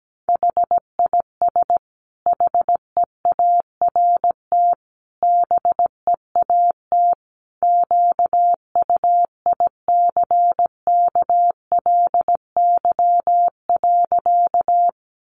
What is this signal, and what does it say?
Morse code, heard as a keyed tone, about 17 wpm, 700 Hz. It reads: HIS HEART BEAT QUICKLY.
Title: Sentences from Top 1300 Words 17wpm